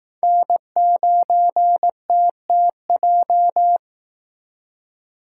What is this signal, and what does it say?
Morse code, heard as a keyed tone, about 18 wpm, 700 Hz. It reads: N9TTJ
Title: Speed Racing - USA Call Signs 15wpm